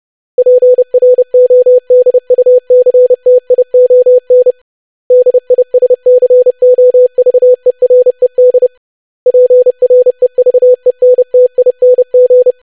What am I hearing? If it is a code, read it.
Morse code, heard as a keyed tone, about 30 wpm, 500 Hz. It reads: PRODUCTION DISCOVERED PREVENTING
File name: lesson / web / longwords-25wpm